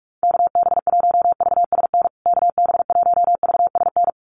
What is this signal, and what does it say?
Morse code, heard as a keyed tone, about 44 wpm, 700 Hz. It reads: KB1VSN KB1VSN
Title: USA Call Signs Repeated 2x 45wpm